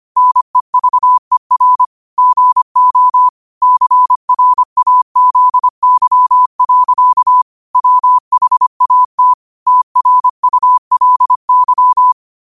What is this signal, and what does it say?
Morse code, heard as a keyed tone, about 25 wpm, 1000 Hz. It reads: NEVER GO CRAZY. WHAT TRULY